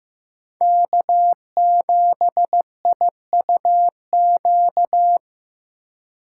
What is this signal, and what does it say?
Morse code, heard as a keyed tone, about 18 wpm, 700 Hz. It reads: K7IUQ
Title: Speed Racing - USA 1x3 Call Signs 15wpm